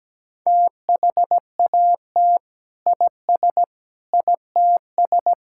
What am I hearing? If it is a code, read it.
Morse code, heard as a keyed tone, about 17 wpm, 700 Hz. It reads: THAT IS ITS
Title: Sentences from Top 1000 Words - Review Part 5 - 17wpm